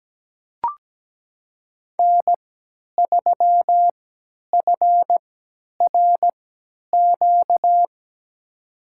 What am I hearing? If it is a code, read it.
Morse code, heard as a keyed tone, about 17 wpm, spaced as for 10 wpm, 700 Hz. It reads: N3FRQ
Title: USA Call Signs - 3x Character Spacing - 17wpm